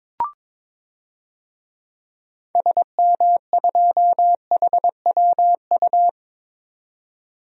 Morse code, heard as a keyed tone, about 22 wpm, 700 Hz. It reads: SM2HWU
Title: DXCC-100 Call Signs - 6 Characters - 22wpm